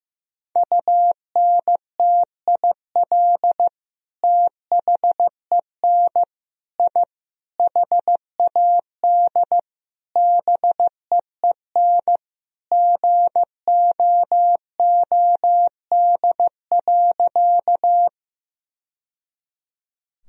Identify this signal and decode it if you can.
Morse code, heard as a keyed tone, about 15 wpm, 700 Hz. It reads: UNTIL THEN I HAD BEEN GOOD.
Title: Sentences from Top 1000 Words - Review Part 1 - 15wpm